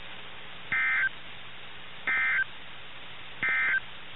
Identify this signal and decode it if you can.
Emergency Alert System (SAME) tones, heard as harsh buzzing bursts. It is heard from 0.6 to 3.9 s.
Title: NOAA Weather Radio - KIH61